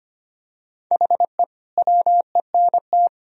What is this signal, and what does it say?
Morse code, heard as a keyed tone, about 25 wpm, 700 Hz. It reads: HE WENT
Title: Sentences from Top 1000 Words - Review Part 2 - 25wpm